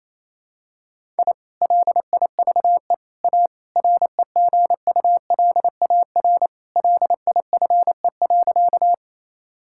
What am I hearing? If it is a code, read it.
Morse code, heard as a keyed tone, about 28 wpm, 700 Hz. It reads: I LIVE A REGULAR LIFE.
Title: Sentences from Top 1000 Words - Review Part 6 - 28wpm